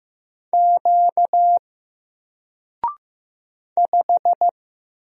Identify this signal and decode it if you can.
Morse code, heard as a keyed tone, about 15 wpm, 700 Hz. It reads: Q 5
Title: Single Letter-Number Alt 15wpm